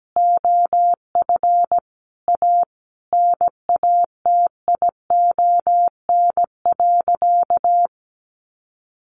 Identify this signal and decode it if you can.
Morse code, heard as a keyed tone, about 17 wpm, 700 Hz. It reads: OF A NATION.
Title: Sentences from Top 1400 Words 17wpm